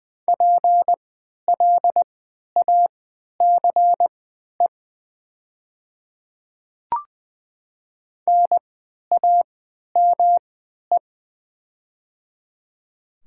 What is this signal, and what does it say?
Morse code, heard as a keyed tone, about 20 wpm, spaced as for 11 wpm, 700 Hz. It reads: PLACE NAME
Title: 3x Character Spacing - Top 300 Words - 20wpm